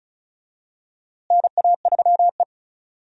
Morse code, heard as a keyed tone, about 34 wpm, 700 Hz. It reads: NA3E
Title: USA Call Signs - 4 Characters - Repeated 3x - 35wpm